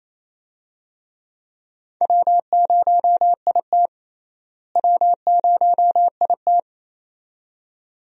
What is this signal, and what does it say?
Morse code, heard as a keyed tone, about 28 wpm, 700 Hz. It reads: W0IT W0IT
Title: USA Call Signs - 4 Characters - Repeated 2x - 28wpm